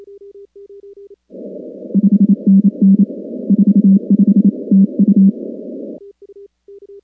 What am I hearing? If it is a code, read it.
Morse code, heard as a keyed tone, about 28 wpm, 200 Hz. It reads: 5NN 45 TU
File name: contests / 2019-cw-wpx-kt5j-k5pi- -2330 0-16